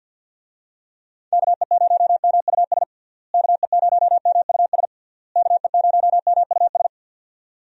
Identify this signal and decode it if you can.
Morse code, heard as a keyed tone, about 51 wpm, 700 Hz. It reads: KE0MUS KE0MUS KE0MUS
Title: USA Call Signs - 6 Characters - Repeated 3x - 50wpm